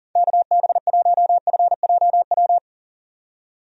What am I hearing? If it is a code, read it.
Morse code, heard as a keyed tone, about 40 wpm, 700 Hz. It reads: KB1FJW
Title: USA Call Signs - 6 Characters - 40wpm